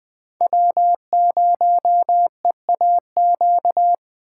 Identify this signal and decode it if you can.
Morse code, heard as a keyed tone, about 20 wpm, 700 Hz. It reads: W0EAQ
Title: Speed Racing - USA 1x3 Call Signs 17wpm